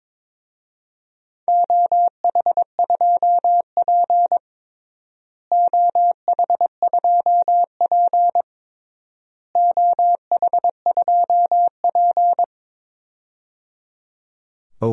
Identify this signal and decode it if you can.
Morse code, heard as a keyed tone, about 21 wpm, 700 Hz. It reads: OH2P OH2P OH2P
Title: DXCC-100 Call Signs - 4 Characters - Repeated 3x - 22wpm